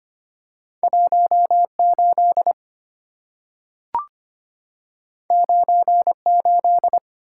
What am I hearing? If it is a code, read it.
Morse code, heard as a keyed tone, about 25 wpm, 700 Hz. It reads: 18 98
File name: Two Letter-Number Combinations - 25wpm